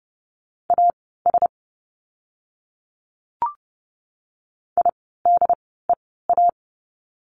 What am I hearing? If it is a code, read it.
Morse code, heard as a keyed tone, about 30 wpm, spaced as for 17 wpm, 700 Hz. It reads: AS IDEA
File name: course / 30:10wpm / 108 Lesson 18f - Letter m - Words 30wpm